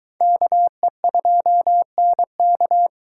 Morse code, heard as a keyed tone, about 22 wpm, 700 Hz. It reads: KE2NK